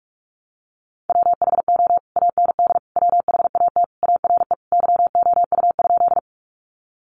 Morse code, heard as a keyed tone, about 44 wpm, 700 Hz. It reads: WHO AND WHAT ARE YOU?
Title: Sentences from Top 100 Words 45wpm